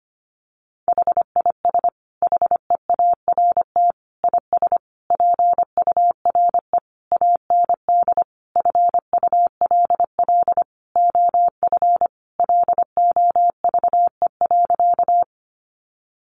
Morse code, heard as a keyed tone, about 25 wpm, 700 Hz. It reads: HIS HEART IS PURE AND FULL OF LOVE.
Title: Sentences from Top 1500 Words 25wpm